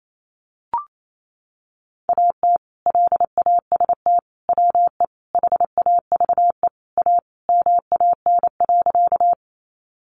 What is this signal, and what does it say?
Morse code, heard as a keyed tone, about 28 wpm, 700 Hz. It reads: AT LAST WE HAVE A MAN.
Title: Sentences from Top 300 Words 28wpm